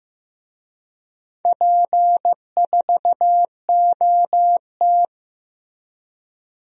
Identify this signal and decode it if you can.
Morse code, heard as a keyed tone, about 15 wpm, 700 Hz. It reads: P4OT
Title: DXCC-100 Call Signs - 4 Characters - Repeated 2x - 15wpm